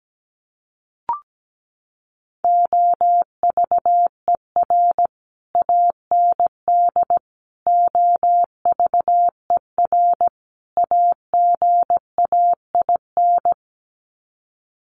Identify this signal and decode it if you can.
Morse code, heard as a keyed tone, about 17 wpm, 700 Hz. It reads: OVER AND OVER AGAIN
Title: Sets of 4 Words 17wpm